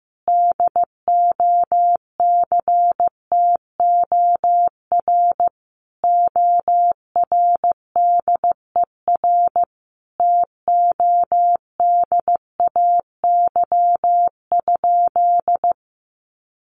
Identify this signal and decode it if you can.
Morse code, heard as a keyed tone, about 15 wpm, 700 Hz. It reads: DOCTOR ORDER TODAY?
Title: Sentences from Top 1500 Words 15wpm